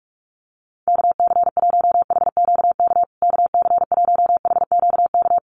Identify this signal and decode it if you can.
Morse code, heard as a keyed tone, about 44 wpm, 700 Hz. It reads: KC1HQK KC1HQK
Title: USA Call Signs Repeated 3x 45wpm